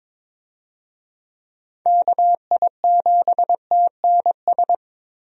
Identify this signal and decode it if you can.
Morse code, heard as a keyed tone, about 22 wpm, 700 Hz. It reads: KI7TNS
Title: USA Call Signs - 6 Characters - Repeated 3x - 22wpm